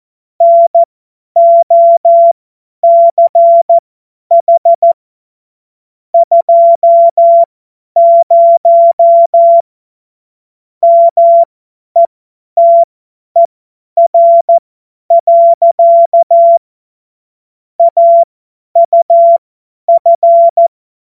Morse code, heard as a keyed tone, about 14 wpm, spaced as for 10 wpm, 680 Hz. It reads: NOCH 20 METER. AUF
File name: Morsekurs / Witze / Witz 30. Dezember 2025 50 F70